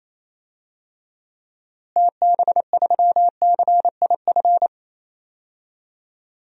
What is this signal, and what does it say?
Morse code, heard as a keyed tone, about 28 wpm, 700 Hz. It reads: TB3CIF
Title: DXCC-100 Call Signs - 6 Characters - Repeated 2x - 28wpm